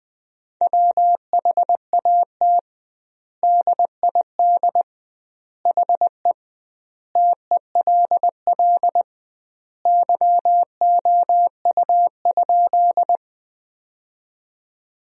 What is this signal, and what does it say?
Morse code, heard as a keyed tone, about 21 wpm, 700 Hz. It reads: WHAT DID HE TELL YOU?
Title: 2x Word-Spacing - Sentences from Top 200 Words - 20wpm